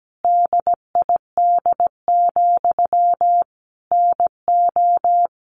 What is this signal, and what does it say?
Morse code, heard as a keyed tone, about 17 wpm, 700 Hz. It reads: DID, NO
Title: Sentences from Top 1500 Words 17wpm